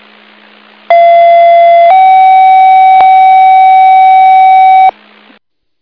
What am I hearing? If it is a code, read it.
Two-tone page, heard as a one-second tone, then a three-second tone, about 680 then 760 Hz, with noise. It is heard from 0.9 to 4.9 s.